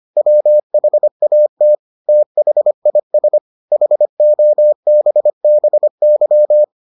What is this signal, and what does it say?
Morse code, heard as a keyed tone, about 25 wpm, 600 Hz. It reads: WHAT THIS HOBBY